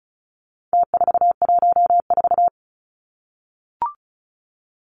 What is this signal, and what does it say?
Morse code, heard as a keyed tone, about 34 wpm, 700 Hz. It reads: T414
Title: Numbers - Contest Style - 4-Digit 35wpm